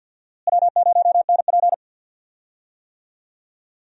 Morse code, heard as a keyed tone, about 52 wpm, 700 Hz. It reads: W0NP
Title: USA Call Signs - 4 Characters - Repeated 2x - 50wpm